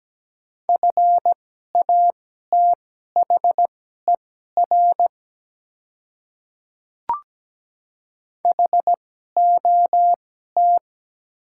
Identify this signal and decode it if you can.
Morse code, heard as a keyed tone, about 17 wpm, spaced as for 12 wpm, 700 Hz. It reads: FATHER HOT
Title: 2x Character Spacing - Top 200 Words - 17wpm